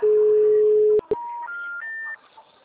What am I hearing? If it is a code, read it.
Special information tone.